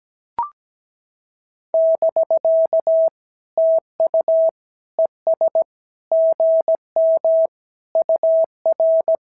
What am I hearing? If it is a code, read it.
Morse code, heard as a keyed tone, about 16 wpm, 650 Hz. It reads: <BK> TU ES GM UR
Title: SOTA QSO Practice 17wpm